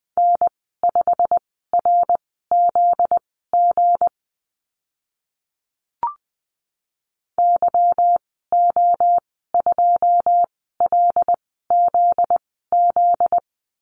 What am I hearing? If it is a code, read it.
Morse code, heard as a keyed tone, about 20 wpm, spaced as for 15 wpm, 700 Hz. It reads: N5RZG YO2LZZ
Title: 245 Lesson 39g - Letter z - Call Signs 20wpm